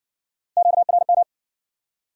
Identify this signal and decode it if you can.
Morse code, heard as a keyed tone, about 59 wpm, 700 Hz. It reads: QRM